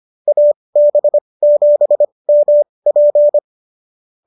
Morse code, heard as a keyed tone, about 25 wpm, spaced as for 20 wpm, 590 Hz. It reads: AB7MP